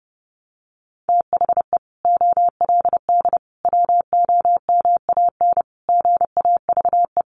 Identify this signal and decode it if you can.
Morse code, heard as a keyed tone, about 30 wpm, 700 Hz. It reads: THE OLD WOMAN GAVE
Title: Sentences from Top 1100 Words 30wpm